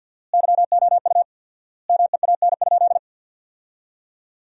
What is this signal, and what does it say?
Morse code, heard as a keyed tone, about 53 wpm, 700 Hz. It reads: YOU MEAN?
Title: 4x Word-Spacing - Sentences from Top 500 Words - 50wpm